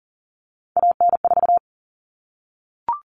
Morse code, heard as a keyed tone, about 40 wpm, 700 Hz. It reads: AN4
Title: All Cut Numbers - 3-Digit 40wpm